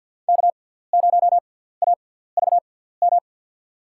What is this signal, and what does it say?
Morse code, heard as a keyed tone, about 52 wpm, spaced as for 17 wpm, 700 Hz. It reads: K0AVM
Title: USA Call Signs - 5 Characters - 6x Character Spacing - 50wpm